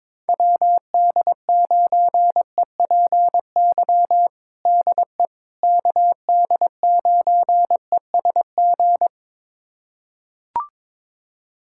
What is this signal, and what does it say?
Morse code, heard as a keyed tone, about 22 wpm, 700 Hz. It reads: WD9EPY DE KD9ESG
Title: USA Callsign DE Callsign 22wpm